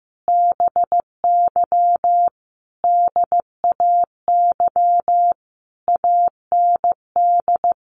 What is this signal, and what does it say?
Morse code, heard as a keyed tone, about 15 wpm, 700 Hz. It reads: BY DAY AND
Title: Sentences from Top 1500 Words 15wpm